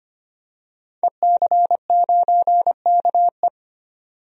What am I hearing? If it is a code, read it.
Morse code, heard as a keyed tone, about 25 wpm, 700 Hz. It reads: EC9KE
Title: DXCC-100 Call Signs - 5 Characters - 25wpm